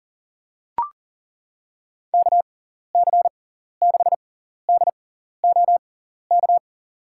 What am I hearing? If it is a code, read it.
Morse code, heard as a keyed tone, about 40 wpm, spaced as for 14 wpm, 700 Hz. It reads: KC6DOK
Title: USA Call Signs - 6 Characters - 6x Character Spacing - 40wpm